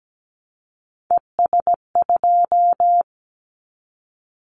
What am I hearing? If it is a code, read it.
Morse code, heard as a keyed tone, about 17 wpm, 700 Hz. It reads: ES2
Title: Callsign Prefixes for Top 100 DXCC - 17wpm